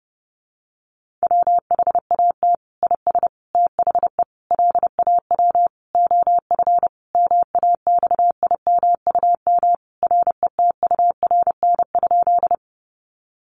Morse code, heard as a keyed tone, about 30 wpm, 700 Hz. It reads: WHAT IS THE LAW OF MAXIMUM RETURN?
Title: Sentences from Top 1300 Words 30wpm